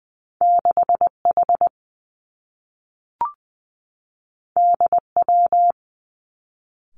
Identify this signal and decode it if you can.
Morse code, heard as a keyed tone, about 20 wpm, 700 Hz. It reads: 6 H D W
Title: Two Challenging Letter-Number Combinations 20wpm